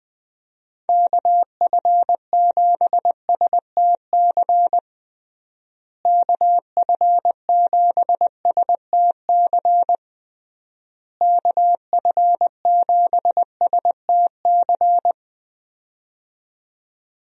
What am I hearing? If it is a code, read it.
Morse code, heard as a keyed tone, about 20 wpm, 700 Hz. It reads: KF7STC KF7STC KF7STC
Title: USA Call Signs - 6 Characters - Repeated 3x - 20wpm